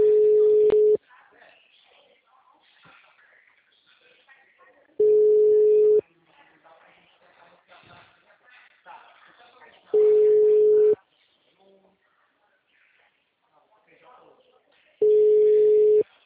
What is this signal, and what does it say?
Ringback tone (Europe, 425 Hz), with speech.
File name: out- -2002- - - .82887